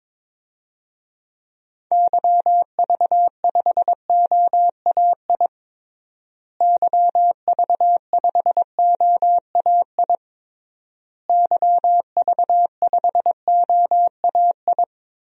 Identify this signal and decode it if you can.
Morse code, heard as a keyed tone, about 22 wpm, 700 Hz. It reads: YV5OAI YV5OAI YV5OAI
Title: DXCC-100 Call Signs - 6 Characters - Repeated 3x - 22wpm